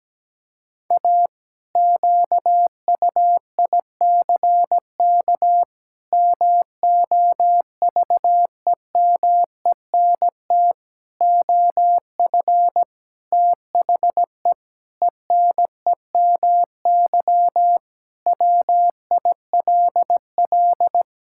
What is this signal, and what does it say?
Morse code, heard as a keyed tone, about 17 wpm, 700 Hz. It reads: A QUICK MOVEMENT OF THE ENEMY WILL
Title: Pangram Challenge v2 - 17wpm